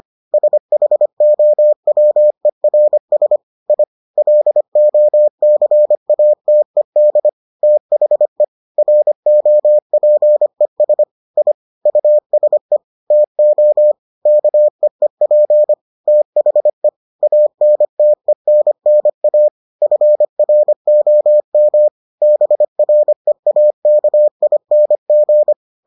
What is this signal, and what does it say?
Morse code, heard as a keyed tone, about 25 wpm, 600 Hz. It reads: SHOWERS I LOCATED THE ROPES I USE TO KEEP THE ANTENNA FROM BREAKING